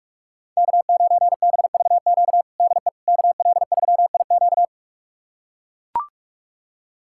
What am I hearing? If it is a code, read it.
Morse code, heard as a keyed tone, about 41 wpm, 700 Hz. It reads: K9BVQ DE KL3IQ
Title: USA Callsign DE Callsign - 5 Characters - 45wpm